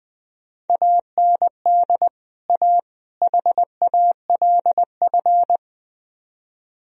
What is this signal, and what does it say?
Morse code, heard as a keyed tone, about 20 wpm, 700 Hz. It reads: AND A HALF
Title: Speed-Racing - Sets of 3 Words 17wpm